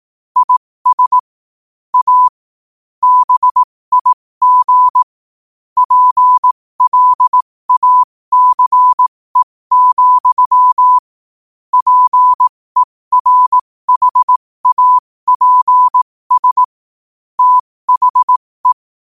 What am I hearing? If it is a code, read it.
Morse code, heard as a keyed tone, about 18 wpm, spaced as for 15 wpm, 1000 Hz. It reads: IS A BIG PLACE, PERHAPS THE
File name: qotdfiles / 15wpm